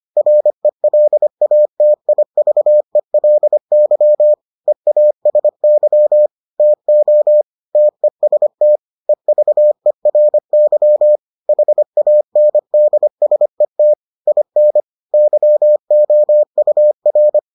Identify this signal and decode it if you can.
Morse code, heard as a keyed tone, about 25 wpm, 600 Hz. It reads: RELATIVELY EASY TO TEST EVERY HANDSET IN YOUR